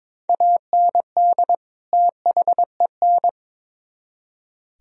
Morse code, heard as a keyed tone, about 26 wpm, 700 Hz. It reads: AND THEN
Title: Speed-Racing - Sets of 2 Words 22wpm